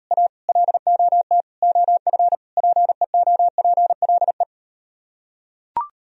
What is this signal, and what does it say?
Morse code, heard as a keyed tone, about 38 wpm, 700 Hz. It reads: A LOT OF PEOPLE